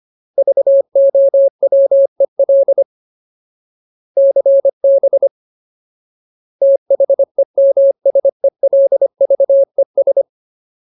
Morse code, heard as a keyed tone, about 25 wpm, 550 Hz. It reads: VOWEL CB THEMSELVES
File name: Ultimate Word List / 25WPM Ultimate